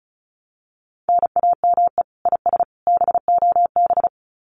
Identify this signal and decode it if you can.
Morse code, heard as a keyed tone, about 34 wpm, 700 Hz. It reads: NAME IS BOB
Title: Common QSO Phrases 35wpm